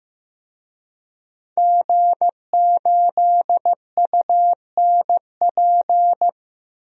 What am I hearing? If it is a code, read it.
Morse code, heard as a keyed tone, about 15 wpm, 700 Hz. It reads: G8UNP
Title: DXCC-100 Call Signs - 5 Characters - 15wpm